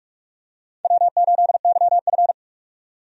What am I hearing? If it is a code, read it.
Morse code, heard as a keyed tone, about 46 wpm, 700 Hz. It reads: W8YF
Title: USA Call Signs - 4 Characters - 45wpm